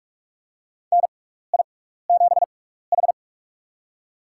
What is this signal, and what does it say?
Morse code, heard as a keyed tone, about 43 wpm, spaced as for 15 wpm, 700 Hz. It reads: NI7H
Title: USA Call Signs - 4 Characters - 6x Character Spacing - 45wpm